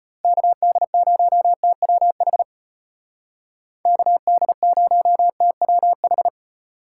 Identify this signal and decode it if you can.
Morse code, heard as a keyed tone, about 33 wpm, 700 Hz. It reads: KD0TWH KD0TWH